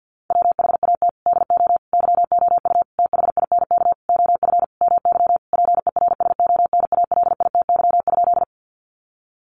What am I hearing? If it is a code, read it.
Morse code, heard as a keyed tone, about 50 wpm, 700 Hz. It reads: WHAT DO YOU THINK OF MY PERSONALITY?
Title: Sentences from Top 1400 Words 50wpm